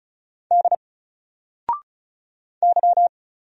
Morse code, heard as a keyed tone, about 35 wpm, 700 Hz. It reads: D Y
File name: Single-Letter-Number-35wpm